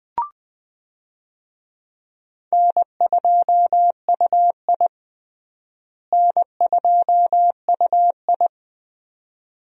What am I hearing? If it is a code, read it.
Morse code, heard as a keyed tone, about 20 wpm, 700 Hz. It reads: N2UI N2UI
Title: USA Call Signs - 4 Characters - Repeated 3x - 20wpm